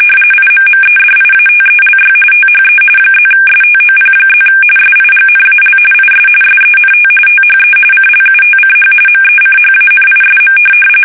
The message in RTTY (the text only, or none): IMMEDIATELY THE +STOP+ SIGNAL IS SENT THE CLUTCH MECHANISM